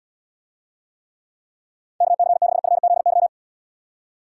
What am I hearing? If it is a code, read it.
Morse code, heard as a keyed tone, about 78 wpm, 700 Hz. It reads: KC6FZY